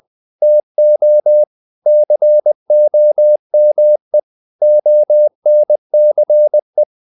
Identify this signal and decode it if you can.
Morse code, heard as a keyed tone, about 20 wpm, 600 Hz. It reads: TO COME ONCE
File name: the old man and the sea part 2 20wpm